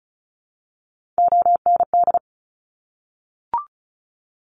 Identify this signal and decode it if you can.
Morse code, heard as a keyed tone, about 36 wpm, 700 Hz. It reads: OND
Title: Top Three Letter Combinations in Words 35wpm